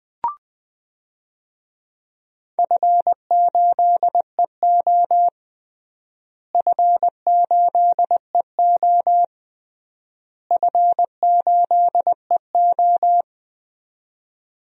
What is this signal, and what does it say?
Morse code, heard as a keyed tone, about 20 wpm, 700 Hz. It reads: F8EO F8EO F8EO
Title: DXCC-100 Call Signs - 4 Characters - Repeated 3x - 20wpm